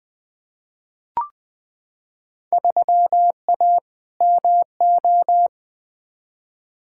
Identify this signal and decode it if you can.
Morse code, heard as a keyed tone, about 20 wpm, 700 Hz. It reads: 3A MO
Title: Field Day Exchanges - 20wpm